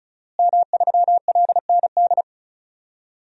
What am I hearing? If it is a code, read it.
Morse code, heard as a keyed tone, about 35 wpm, 700 Hz. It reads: M3LND
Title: DXCC-100 Call Signs - 5 Characters - Repeated 3x - 35wpm